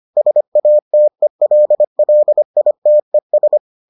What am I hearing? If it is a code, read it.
Morse code, heard as a keyed tone, about 25 wpm, 600 Hz. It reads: SATELLITES